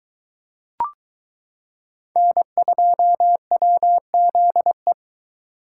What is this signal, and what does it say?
Morse code, heard as a keyed tone, about 21 wpm, 700 Hz. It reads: N2WZE